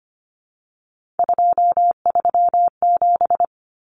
Morse code, heard as a keyed tone, about 25 wpm, 700 Hz. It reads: 237
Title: Numbers - 3-Digit 25wpm